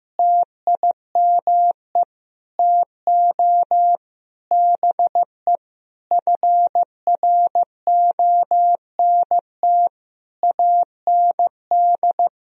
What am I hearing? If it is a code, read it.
Morse code, heard as a keyed tone, about 14 wpm, 700 Hz. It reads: TIME TO BE FRONT AND
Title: Sentences from Top 1000 Words - Review Part 2 - 15wpm